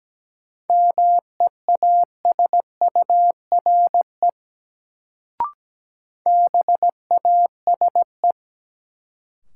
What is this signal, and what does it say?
Morse code, heard as a keyed tone, about 17 wpm, 700 Hz. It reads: MEASURE BASE
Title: Sentences from Top 1000 Words - Review Part 1 - New Word Drill - 17wpm